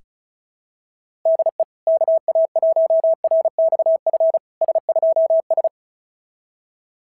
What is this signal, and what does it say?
Morse code, heard as a keyed tone, about 36 wpm, 690 Hz. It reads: DE KA1RXF S2S
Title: SOTA QSO Practice 35wpm